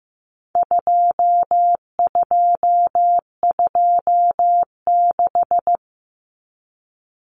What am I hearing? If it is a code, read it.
Morse code, heard as a keyed tone, about 15 wpm, 700 Hz. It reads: 2226
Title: Common Cut Numbers - 4-Digit 15wpm